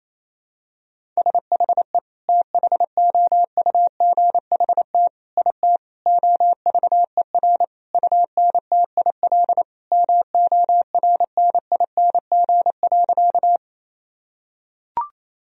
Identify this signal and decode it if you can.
Morse code, heard as a keyed tone, about 28 wpm, 700 Hz. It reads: SHE THOUGHT IT OVER UNTIL MORNING.
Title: Sentences from Top 1000 Words - Review Part 1 - 28wpm